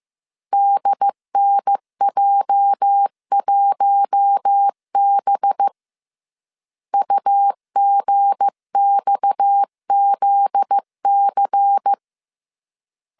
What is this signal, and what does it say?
Morse code, heard as a keyed tone, about 15 wpm, 800 Hz. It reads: DNJ1B UGXZC
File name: CW / DET / 1 9 1500C S C 14wpm 3245